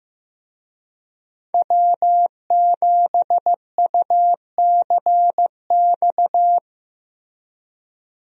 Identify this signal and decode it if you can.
Morse code, heard as a keyed tone, about 15 wpm, 700 Hz. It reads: W7UCX
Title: USA Call Signs - 5 Characters - Repeated 3x - 15wpm